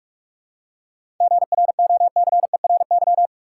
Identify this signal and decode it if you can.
Morse code, heard as a keyed tone, about 44 wpm, 700 Hz. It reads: GROCERY
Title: Seven Letter Words - 45wpm